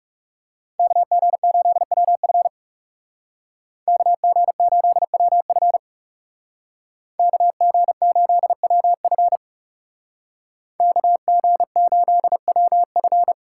Morse code, heard as a keyed tone, about 36 wpm, 700 Hz. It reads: KG8WF KG8WF KG8WF KG8WF
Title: Speed Racing - USA 2x2 Call Signs 30wpm